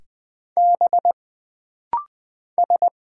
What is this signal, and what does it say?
Morse code, heard as a keyed tone, about 20 wpm, 700 Hz. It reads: B S
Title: Single Letters - Rapid-Fire 20wpm